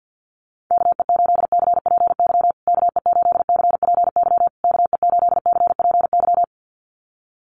Morse code, heard as a keyed tone, about 48 wpm, 700 Hz. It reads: KE8CPY KE8CPY KE8CPY
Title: USA Call Signs Repeated 3x 50wpm